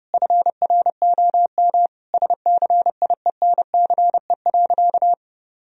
Morse code, heard as a keyed tone, about 30 wpm, 700 Hz. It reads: FROM SCIENCE.
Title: Sentences from Top 1000 Words - Review Part 2 - 30wpm